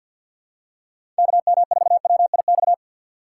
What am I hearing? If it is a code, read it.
Morse code, heard as a keyed tone, about 53 wpm, 700 Hz. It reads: KM4WIX
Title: USA Call Signs - 6 Characters - 4x Character Spacing - 50wpm